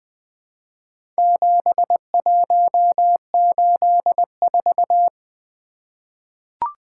Morse code, heard as a keyed tone, about 20 wpm, 700 Hz. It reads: 7184
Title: Four Letter-Number Combinations - 20wpm